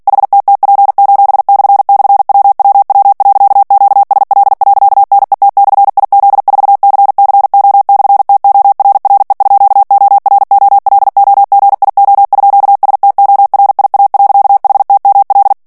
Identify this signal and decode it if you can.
Morse code, heard as a keyed tone, about 48 wpm, 800 Hz. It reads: HTTP://WWW.QSL.NET/IZ4CCO/TORRE_OROLOGIO_STORIA.HTML
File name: morse mp3 / ITALIEN texte clair 1 vitesse 45 mots minute